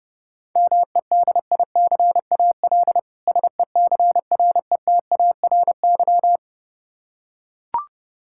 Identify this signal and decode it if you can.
Morse code, heard as a keyed tone, about 30 wpm, 700 Hz. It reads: MEDICAL SECRETARY
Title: K1USN SST QSO Practice 30wpm